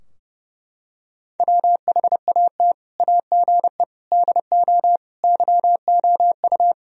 Morse code, heard as a keyed tone, about 30 wpm, 700 Hz. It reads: WHAT AGE DO YOU
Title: Sentences from Top 1000 Words - Review Part 3 - 30wpm